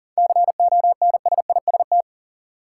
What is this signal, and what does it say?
Morse code, heard as a keyed tone, about 40 wpm, 700 Hz. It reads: CONSIST
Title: Sentences from Top 1400 Words - New Word Drill Alt - 40wpm